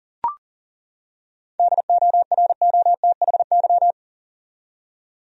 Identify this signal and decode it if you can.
Morse code, heard as a keyed tone, about 40 wpm, 700 Hz. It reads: DOROTHY
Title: USA Common Names - 40wpm